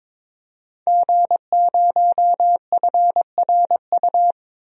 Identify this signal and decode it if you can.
Morse code, heard as a keyed tone, about 22 wpm, 700 Hz. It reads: G0FRU
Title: DXCC-100 Call Signs - 5 Characters - 22wpm